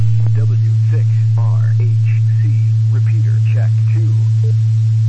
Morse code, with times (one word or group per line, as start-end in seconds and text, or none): none